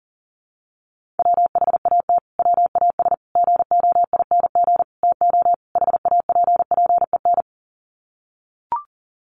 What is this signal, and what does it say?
Morse code, heard as a keyed tone, about 40 wpm, 700 Hz. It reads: WHAT WAS GOING TO HAPPEN
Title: Sets of 5 Words Encore 40wpm